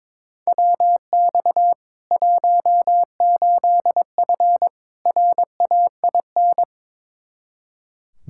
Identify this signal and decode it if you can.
Morse code, heard as a keyed tone, about 21 wpm, 700 Hz. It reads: WX 18F RAIN
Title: Practice QSO - Exchange 2 - Info - 22wpm